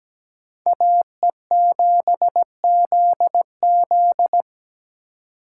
Morse code, heard as a keyed tone, about 20 wpm, 700 Hz. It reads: AE7ZZ
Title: Speed Racing - USA Call Signs 17wpm